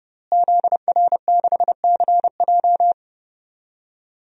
Morse code, heard as a keyed tone, about 30 wpm, 700 Hz. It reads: ZR6CJ
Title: DXCC-100 Call Signs - 5 Characters - Repeated 2x - 30wpm